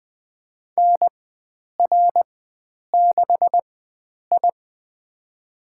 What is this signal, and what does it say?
Morse code, heard as a keyed tone, about 20 wpm, spaced as for 9 wpm, 700 Hz. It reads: NR6I
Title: USA Call Signs - 4 Characters - 4x Character Spacing - 20wpm